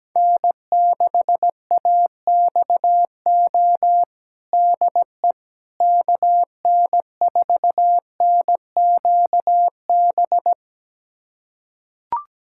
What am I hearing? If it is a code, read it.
Morse code, heard as a keyed tone, about 17 wpm, 700 Hz. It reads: N6AXO DE KN4NQB
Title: USA Callsign DE Callsign 17wpm